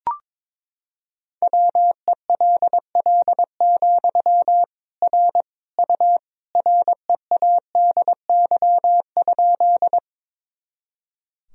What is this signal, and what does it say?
Morse code, heard as a keyed tone, about 22 wpm, 700 Hz. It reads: WELL, R U READY?
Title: Sentences from Top 400 Words and Abbreviations 22wpm